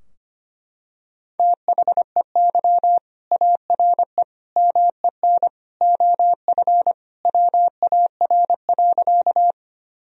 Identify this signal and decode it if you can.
Morse code, heard as a keyed tone, about 25 wpm, 700 Hz. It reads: THEY ARE MEN OF WAR.
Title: Sentences from Top 1000 Words - Review Part 2 - 25wpm